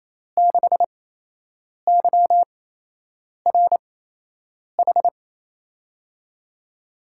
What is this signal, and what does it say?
Morse code, heard as a keyed tone, about 28 wpm, spaced as for 8 wpm, 700 Hz. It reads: 6YRH
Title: USA Call Signs - 6 Characters - 8x Character Spacing - 28wpm